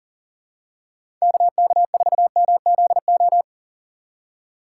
Morse code, heard as a keyed tone, about 40 wpm, 700 Hz. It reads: KK4MZO
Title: USA Call Signs - 6 Characters - 6x Character Spacing - 40wpm